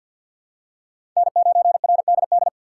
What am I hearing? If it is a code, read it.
Morse code, heard as a keyed tone, about 48 wpm, 700 Hz. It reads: N9RDD
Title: USA Call Signs - 5 Characters - 6x Character Spacing - 50wpm